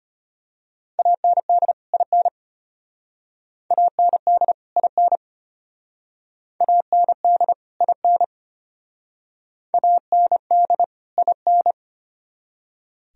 Morse code, heard as a keyed tone, about 32 wpm, 700 Hz. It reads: AND IN AND IN AND IN AND IN